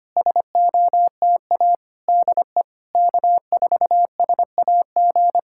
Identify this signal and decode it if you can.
Morse code, heard as a keyed tone, about 26 wpm, 700 Hz. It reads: SOTA DE K4SAG
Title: SOTA QSO Practice 25wpm